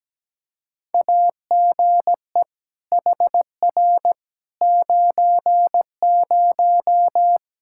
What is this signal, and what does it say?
Morse code, heard as a keyed tone, about 17 wpm, 700 Hz. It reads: AGE HR 90
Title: Practice QSO - Exchange 3 - More Info - 17wpm